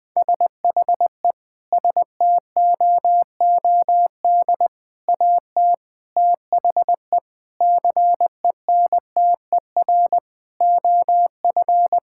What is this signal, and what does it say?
Morse code, heard as a keyed tone, about 20 wpm, 700 Hz. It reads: SHE STOOD AT THE CENTER OF
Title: Sentences with 5-Word Phrases - 20wpm